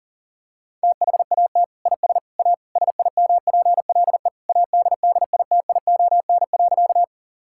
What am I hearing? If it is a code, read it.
Morse code, heard as a keyed tone, about 40 wpm, 700 Hz. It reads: THAT IS A SIMPLE ADDITION.
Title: Sentences from Top 1000 Words - Review Part 6 - 40wpm